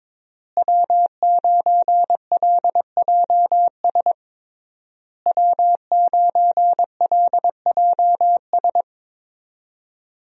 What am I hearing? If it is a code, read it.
Morse code, heard as a keyed tone, about 21 wpm, 700 Hz. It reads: W9LJS W9LJS
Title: USA Call Signs - 5 Characters - Repeated 3x - 22wpm